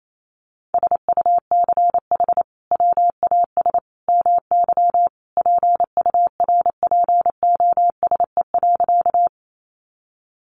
Morse code, heard as a keyed tone, about 28 wpm, 700 Hz. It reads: SUCH WAS MY PURPOSE.
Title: Sentences from Top 800 Words 28wpm